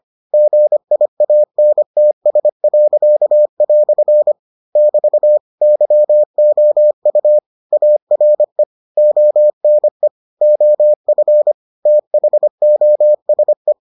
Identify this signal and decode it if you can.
Morse code, heard as a keyed tone, about 25 wpm, 600 Hz. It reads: GIANTS." <BT> YOU ARE ONE OF THOSE